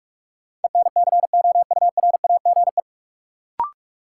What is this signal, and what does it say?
Morse code, heard as a keyed tone, about 47 wpm, 700 Hz. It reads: ENCOURAGE
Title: Sentences from Top 1400 Words - New Word Drill - 45wpm